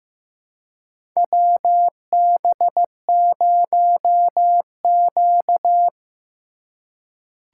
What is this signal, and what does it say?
Morse code, heard as a keyed tone, about 15 wpm, 700 Hz. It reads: WB0Q
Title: USA Call Signs - 4 Characters - 4x Character Spacing - 15wpm